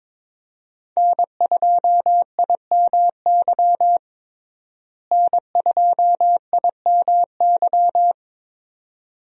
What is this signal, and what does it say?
Morse code, heard as a keyed tone, about 22 wpm, 700 Hz. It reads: N2IMY N2IMY
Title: USA Call Signs - 5 Characters - Repeated 2x - 22wpm